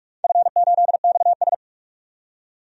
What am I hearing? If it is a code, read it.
Morse code, heard as a keyed tone, about 44 wpm, 700 Hz. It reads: F8XS
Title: DXCC-100 Call Signs - 4 Characters - 45wpm